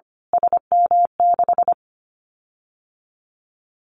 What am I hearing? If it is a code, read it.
Morse code, heard as a keyed tone, about 25 wpm, 700 Hz. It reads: SM6
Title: Callsign Prefixes for Top 25 DXCC - Repeated 3x - 25wpm